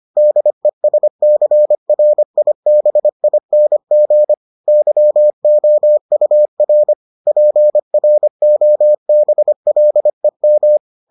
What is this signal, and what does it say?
Morse code, heard as a keyed tone, about 25 wpm, 600 Hz. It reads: DESCRIBING YOUR PROBLEM